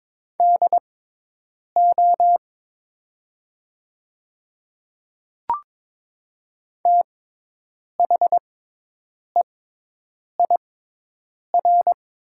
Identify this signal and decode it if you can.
Morse code, heard as a keyed tone, about 22 wpm, spaced as for 8 wpm, 700 Hz. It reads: DO THEIR
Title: 6x Character Spacing - Top 100 Words - 22wpm